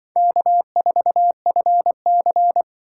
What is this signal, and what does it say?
Morse code, heard as a keyed tone, about 25 wpm, 700 Hz. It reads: K4FC
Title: Speed Racing - USA 1x2 Call Signs 20wpm